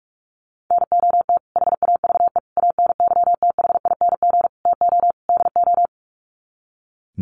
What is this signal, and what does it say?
Morse code, heard as a keyed tone, about 44 wpm, 700 Hz. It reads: NOT HAVE ANYTHING TO DO
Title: Sets of 5 Words Encore 45wpm